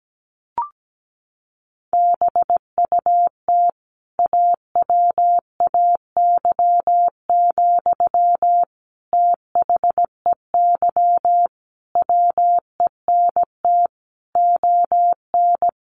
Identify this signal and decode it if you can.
Morse code, heard as a keyed tone, about 17 wpm, 700 Hz. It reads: BUT AWAY, THEY WENT ON
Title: Sentences from Top 1400 Words 17wpm